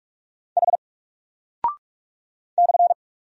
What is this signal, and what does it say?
Morse code, heard as a keyed tone, about 44 wpm, 700 Hz. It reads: H /
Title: Single Character (Letter:Number:Punctuation:Prosign) 45wpm